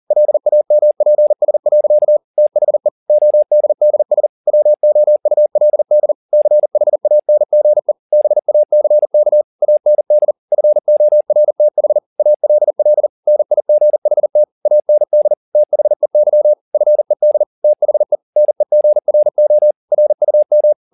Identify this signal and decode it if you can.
Morse code, heard as a keyed tone, about 40 wpm, 600 Hz. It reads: LAMPS. THE ODDS WOULD CHANGE BACK AND FORTH ALL NIGHT AND THEY FED THE NEGRO RUM
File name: the old man and the sea 2 40wpm